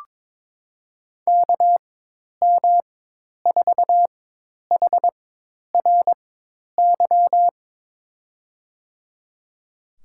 Morse code, heard as a keyed tone, about 22 wpm, spaced as for 10 wpm, 700 Hz. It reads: KM4HRY
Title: USA Call Signs - 6 Characters - 4x Character Spacing - 22wpm